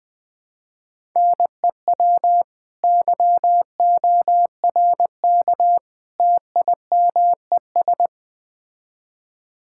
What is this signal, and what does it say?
Morse code, heard as a keyed tone, about 21 wpm, 700 Hz. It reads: NEW YORK TIMES
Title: Build A Phrase - 3-Word Phrases - 20wpm